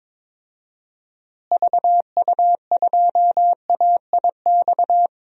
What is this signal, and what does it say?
Morse code, heard as a keyed tone, about 22 wpm, 700 Hz. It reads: VU2AIX
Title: DXCC-100 Call Signs - 6 Characters - Repeated 2x - 22wpm